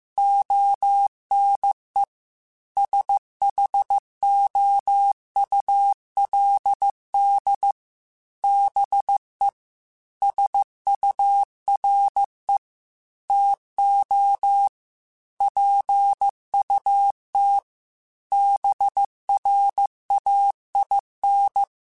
Morse code, heard as a keyed tone, about 15 wpm, 800 Hz. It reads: ONE SHOULD BE SURE TO PUT BRAIN
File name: Chuckles / Chuckles-mp3s / BrainMouth @13